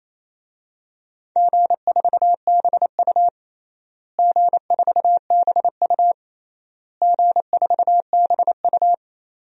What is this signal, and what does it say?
Morse code, heard as a keyed tone, about 28 wpm, 700 Hz. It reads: G4BU G4BU G4BU
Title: DXCC-100 Call Signs - 4 Characters - Repeated 3x - 28wpm